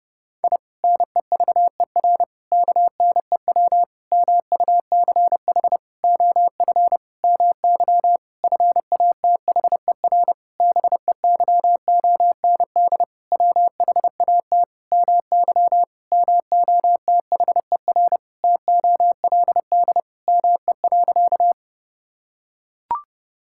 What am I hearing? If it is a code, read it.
Morse code, heard as a keyed tone, about 30 wpm, 700 Hz. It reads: I NEVER KNEW MUCH OF MY FATHER BEYOND WHAT MY MOTHER TOLD ME.
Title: Sentences from Top 1000 Words - Review Part 6 - 30wpm